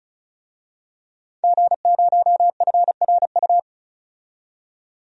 Morse code, heard as a keyed tone, about 34 wpm, 700 Hz. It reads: G0FRU
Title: DXCC-100 Call Signs - 5 Characters - 35wpm